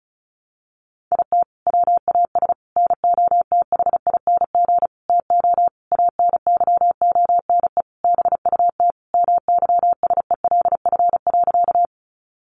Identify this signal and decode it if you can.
Morse code, heard as a keyed tone, about 34 wpm, 700 Hz. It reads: IT WAS NOTHING TO ANYONE BUT MYSELF.
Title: Sentences from Top 1300 Words 35wpm